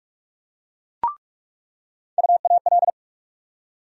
Morse code, heard as a keyed tone, about 47 wpm, 700 Hz. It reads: UAL
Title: USA Call Sign Postfixes - 3 Letters 45wpm